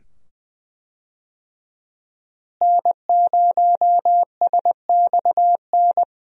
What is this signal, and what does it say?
Morse code, heard as a keyed tone, about 20 wpm, 700 Hz. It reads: N0SXN
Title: USA Call Signs - 5 Characters - Repeated 2x - 20wpm